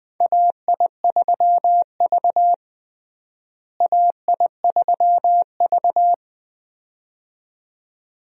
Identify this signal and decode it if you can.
Morse code, heard as a keyed tone, about 20 wpm, 700 Hz. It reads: AI3V AI3V
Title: USA Call Signs - 4 Characters - Repeated 2x - 20wpm